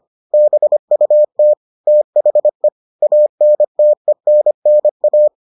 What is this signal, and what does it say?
Morse code, heard as a keyed tone, about 25 wpm, 600 Hz. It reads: BUT THE ANTENNA